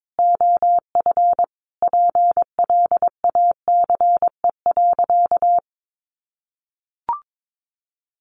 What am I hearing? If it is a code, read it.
Morse code, heard as a keyed tone, about 22 wpm, 700 Hz. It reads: OF PLACE.
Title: Sentences from Top 1400 Words 22wpm